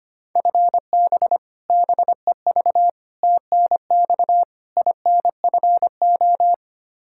Morse code, heard as a keyed tone, about 25 wpm, 700 Hz. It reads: FB BEV TNX INFO
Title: Practice QSO - Exchange 3 - More Info - 25wpm